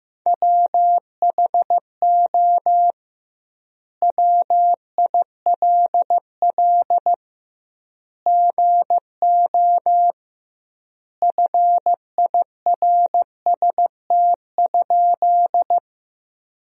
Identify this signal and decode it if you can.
Morse code, heard as a keyed tone, about 15 wpm, 700 Hz. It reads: WHO WILL GO FIRST?
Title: 2x Word-Spacing - Sentences from Top 100 Words - 15wpm